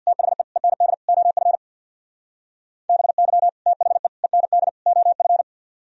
Morse code, heard as a keyed tone, about 64 wpm, 700 Hz. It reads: THE END OF BY THE END OF